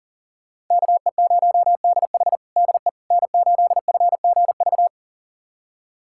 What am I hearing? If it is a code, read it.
Morse code, heard as a keyed tone, about 40 wpm, 700 Hz. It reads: KE0DH DE N8FGV
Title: USA Callsign DE Callsign - 5 Characters - 40wpm